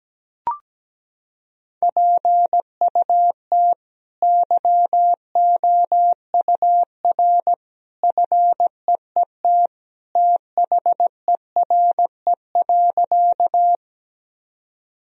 Morse code, heard as a keyed tone, about 17 wpm, 700 Hz. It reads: PUT YOUR FEET THERE.
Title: Sentences from Top 1000 Words - Review Part 1 - 17wpm